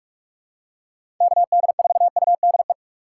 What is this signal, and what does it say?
Morse code, heard as a keyed tone, about 47 wpm, 700 Hz. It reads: KD4UDE
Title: USA Call Signs - 6 Characters - 2x Character Spacing - 45wpm